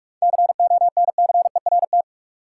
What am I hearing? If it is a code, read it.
Morse code, heard as a keyed tone, about 46 wpm, 700 Hz. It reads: CONCERT
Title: Seven Letter Words - 45wpm